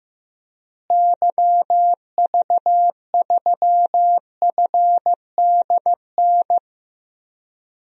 Morse code, heard as a keyed tone, about 15 wpm, 700 Hz. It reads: YV3FDN
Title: DXCC-100 Call Signs - 6 Characters - Repeated 3x - 15wpm